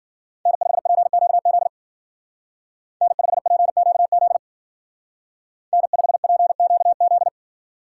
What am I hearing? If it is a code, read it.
Morse code, heard as a keyed tone, about 49 wpm, 700 Hz. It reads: N5PQZ N5PQZ N5PQZ
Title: Speed Racing - USA 1x3 Call Signs 40wpm